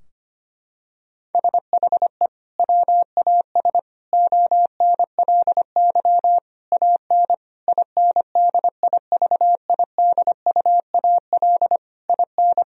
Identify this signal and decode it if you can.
Morse code, heard as a keyed tone, about 25 wpm, 700 Hz. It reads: SHE WAS ONLY AN INDIVIDUAL IN
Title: Sentences from Top 1000 Words - Review Part 4 - 25wpm